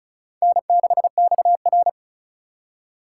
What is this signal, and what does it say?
Morse code, heard as a keyed tone, about 36 wpm, 700 Hz. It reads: N6XR
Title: USA Call Signs - 2x Character Spacing - 35wpm